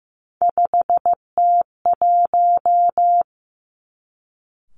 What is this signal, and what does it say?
Morse code, heard as a keyed tone, about 15 wpm, 700 Hz. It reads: 5T1
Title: Three Challenging Letter-Number Combinations 15wpm